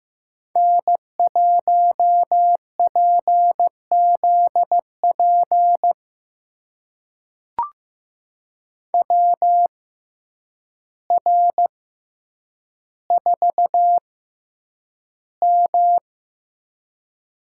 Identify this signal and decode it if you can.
Morse code, heard as a keyed tone, about 15 wpm, spaced as for 8 wpm, 700 Hz. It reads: N1PZP WR4M
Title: USA Call Signs - 5 Characters - 6x Character Spacing - 15wpm